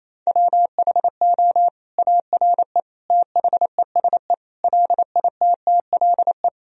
Morse code, heard as a keyed tone, about 28 wpm, 700 Hz. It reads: WHO ARE THESE LITTLE
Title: Sentences from Top 1000 Words - Review Part 1 - 28wpm